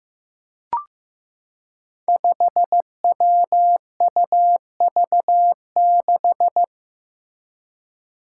Morse code, heard as a keyed tone, about 15 wpm, 700 Hz. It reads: 5WUV6
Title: Five Challenging Letter-Number Combinations 15wpm